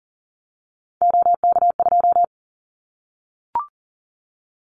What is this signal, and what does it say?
Morse code, heard as a keyed tone, about 40 wpm, 700 Hz. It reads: OK2
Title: Callsign Prefixes for Top 100 DXCC - 40wpm